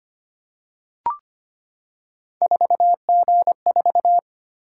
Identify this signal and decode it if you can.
Morse code, heard as a keyed tone, about 25 wpm, 700 Hz. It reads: 4G4
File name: Three Letter-Number Combinations - 25wpm